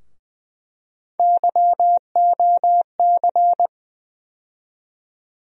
Morse code, heard as a keyed tone, about 20 wpm, 700 Hz. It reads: YOC
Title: USA Call Sign Postfixes - 3 Letters 20wpm